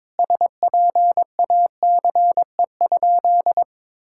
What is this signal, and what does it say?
Morse code, heard as a keyed tone, about 22 wpm, 700 Hz. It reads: SPACE?
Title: Sentences from Top 1000 Words - Review Part 5 - 22wpm